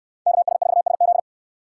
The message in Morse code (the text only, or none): KS4SZ